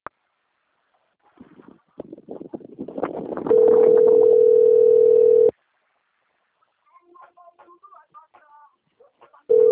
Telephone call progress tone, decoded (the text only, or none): ringback tone